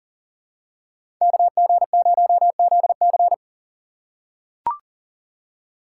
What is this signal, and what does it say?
Morse code, heard as a keyed tone, about 40 wpm, 700 Hz. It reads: KG0ZC
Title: USA Call Signs - 4x Character Spacing - 40wpm